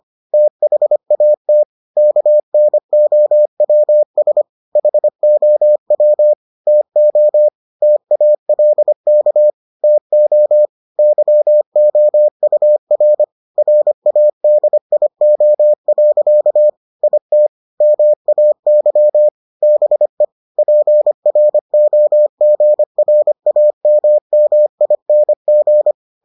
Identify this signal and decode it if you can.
Morse code, heard as a keyed tone, about 25 wpm, 600 Hz. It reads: THAT KNOWS HOW TO TALK TO YOUR RADIO. IT MAY BE PROGRAMMING